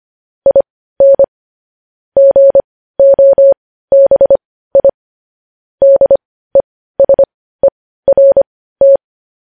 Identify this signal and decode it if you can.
Morse code, heard as a keyed tone, about 25 wpm, spaced as for 15 wpm, 550 Hz. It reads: IN GOBI DESERT
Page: news-headlines-in-morse-code-at-15-wpm